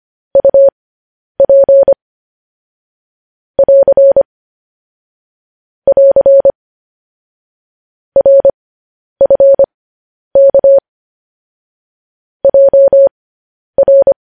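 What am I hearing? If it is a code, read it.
Morse code, heard as a keyed tone, about 25 wpm, spaced as for 10 wpm, 550 Hz. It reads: UP <AR> <AR> RFK JR
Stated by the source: Morse code news headlines broadcast at 10 wpm.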